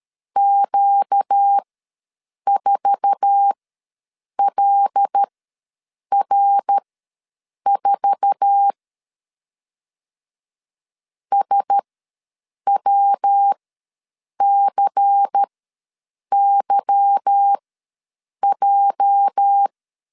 Morse code, heard as a keyed tone, about 13 wpm, spaced as for 7 wpm, 800 Hz. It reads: Q4LR4 SWCYJ
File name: LEZIONI / 8 YJ45 / 8 YJ4 ws 12wpm 3249